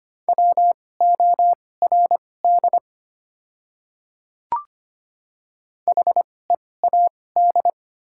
Morse code, heard as a keyed tone, about 25 wpm, spaced as for 18 wpm, 690 Hz. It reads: WORD HEAD
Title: 2x Character Spacing - Top 300 Words - 25wpm